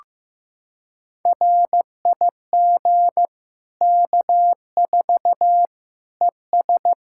Morse code, heard as a keyed tone, about 15 wpm, 700 Hz. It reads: RIG K4 ES
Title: K1USN SST QSO Practice 15wpm